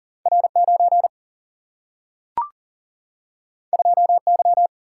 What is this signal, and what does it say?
Morse code, heard as a keyed tone, about 40 wpm, 700 Hz. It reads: R 9 2 Y
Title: Single-Letter Single-Number 40wpm